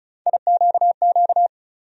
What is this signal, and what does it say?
Morse code, heard as a keyed tone, about 36 wpm, 700 Hz. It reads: IQQ
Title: USA Call Sign Postfixes - 3 Letters 35wpm